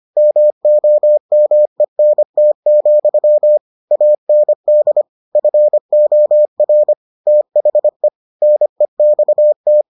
Morse code, heard as a keyed tone, about 25 wpm, 600 Hz. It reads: MOMENT, AND FOR THE NEXT